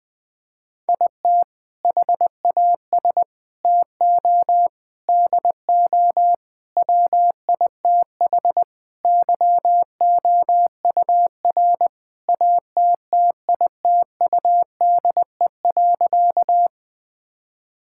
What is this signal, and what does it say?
Morse code, heard as a keyed tone, about 20 wpm, 700 Hz. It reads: IT HAS TO DO WITH YOUR ATTITUDE.
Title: Sentences with 5-Word Phrases - 20wpm